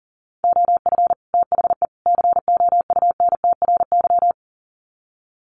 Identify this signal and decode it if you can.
Morse code, heard as a keyed tone, about 37 wpm, 700 Hz. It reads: OF THE COUNTRY
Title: Sets of 3 Words 40wpm